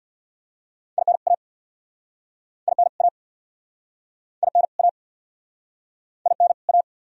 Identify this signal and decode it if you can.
Morse code, heard as a keyed tone, about 65 wpm, 700 Hz. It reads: IN A IN A IN A IN A